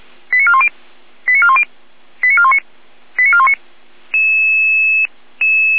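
Five-tone selective call (ZVEI), heard as quick runs of five tones, tones of about 70 ms. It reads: 89410 89410 89419 89419
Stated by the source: ZVEI five-tone selective call 89265 received by SDR, archived on the feuerwehr-orlamuende fire brigade server.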